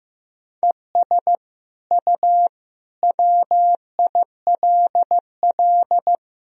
Morse code, heard as a keyed tone, about 15 wpm, 700 Hz. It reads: ES U WILL
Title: Sentences from Top 400 Words and Abbreviations 15wpm